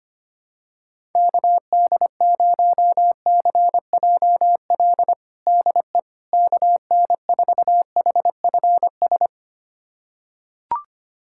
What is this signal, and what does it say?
Morse code, heard as a keyed tone, about 25 wpm, 700 Hz. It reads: KD0CJL DE KN4HFS
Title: USA Callsign DE Callsign 25wpm